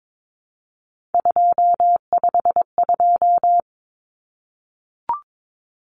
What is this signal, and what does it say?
Morse code, heard as a keyed tone, about 22 wpm, 700 Hz. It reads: 252
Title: Common Cut Numbers - 3-Digit 22wpm